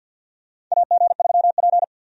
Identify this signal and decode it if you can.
Morse code, heard as a keyed tone, about 51 wpm, 700 Hz. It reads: AG3P
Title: USA Call Signs - 4 Characters - 50wpm